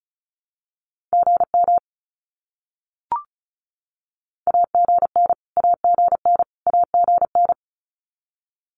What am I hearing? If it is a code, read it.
Morse code, heard as a keyed tone, about 34 wpm, 700 Hz. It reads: GM AGN AGN AGN
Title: Abbreviations Repeated 3x - 35wpm